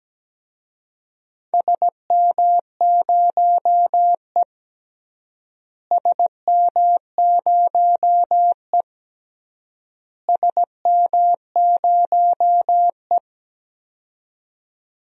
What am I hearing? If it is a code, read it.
Morse code, heard as a keyed tone, about 17 wpm, 700 Hz. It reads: SM0E SM0E SM0E
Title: DXCC-100 Call Signs - 4 Characters - Repeated 3x - 17wpm